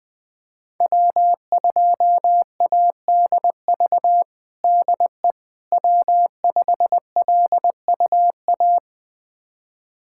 Morse code, heard as a keyed tone, about 20 wpm, 700 Hz. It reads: W2ADV DE W5LUA
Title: USA Callsign DE Callsign - 5 Characters - 20wpm